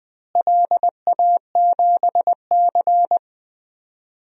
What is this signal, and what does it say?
Morse code, heard as a keyed tone, about 20 wpm, 700 Hz. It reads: LA7C
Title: DXCC-100 Call Signs - 4 Characters - Repeated 2x - 20wpm